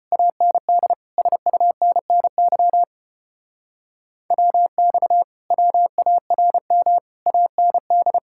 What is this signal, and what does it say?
Morse code, heard as a keyed tone, about 29 wpm, 700 Hz. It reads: AND SUNNY WX WARM AND
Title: Speed-Racing Common QSO Phrases - 25wpm